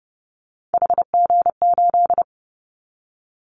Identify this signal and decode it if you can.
Morse code, heard as a keyed tone, about 30 wpm, 700 Hz. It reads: HG8
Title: Callsign Prefixes for Top 25 DXCC - Repeated 3x - 30wpm